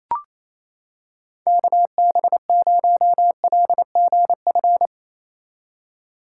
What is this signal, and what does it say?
Morse code, heard as a keyed tone, about 28 wpm, 700 Hz. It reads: KB0LGF
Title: K1USN SST QSO Practice - 28wpm